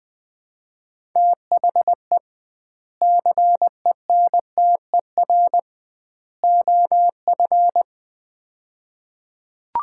Morse code, heard as a keyed tone, about 20 wpm, 700 Hz. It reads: THE CENTER OF
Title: 2x Word-Spacing - Sets of 3 Words - 20wpm